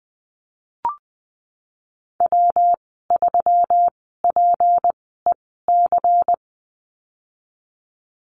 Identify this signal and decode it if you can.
Morse code, heard as a keyed tone, about 20 wpm, spaced as for 14 wpm, 700 Hz. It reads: W3PEC
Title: 167 Lesson 27g - Number 7 - Call Signs 20wpm